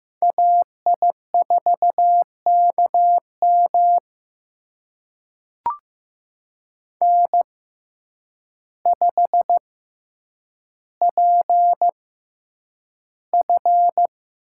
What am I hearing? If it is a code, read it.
Morse code, heard as a keyed tone, about 15 wpm, spaced as for 8 wpm, 700 Hz. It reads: AI4KM N5PF
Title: USA Call Signs - 5 Characters - 6x Character Spacing - 15wpm